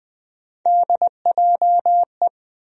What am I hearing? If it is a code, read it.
Morse code, heard as a keyed tone, about 21 wpm, 700 Hz. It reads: DJE
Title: USA Call Sign Postfixes - 3 Letters 20wpm